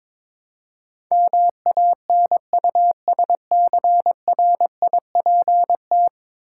Morse code, heard as a keyed tone, about 23 wpm, 700 Hz. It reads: MANUSCRIPT
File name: practice / 22wpm / 10 Letter Words - First 3 Letters Given - 22wpm